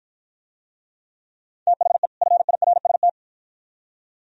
Morse code, heard as a keyed tone, about 49 wpm, 700 Hz. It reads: THE FIRST